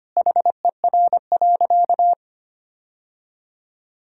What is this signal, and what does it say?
Morse code, heard as a keyed tone, about 25 wpm, 700 Hz. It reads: HER.
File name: practice / 25wpm / Sentences with 4-Word Phrases - 25wpm